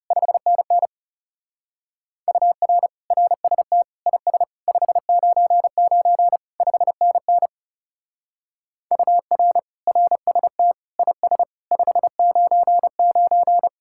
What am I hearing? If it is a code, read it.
Morse code, heard as a keyed tone, about 35 wpm, 700 Hz. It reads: 5NN UR RST IS 599 5NN UR RST IS 599
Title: Speed-Racing Common QSO Phrases - 30wpm